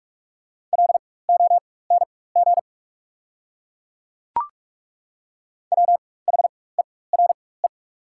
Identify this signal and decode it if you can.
Morse code, heard as a keyed tone, about 43 wpm, spaced as for 21 wpm, 690 Hz. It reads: LONG WHERE
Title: 4x Character Spacing - Top 300 Words - 45wpm